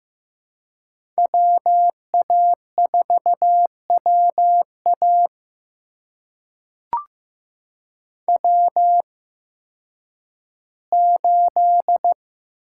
Morse code, heard as a keyed tone, about 15 wpm, spaced as for 10 wpm, 700 Hz. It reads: WA4WA W8
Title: USA Call Signs - 5 Characters - 8x Character Spacing - 15wpm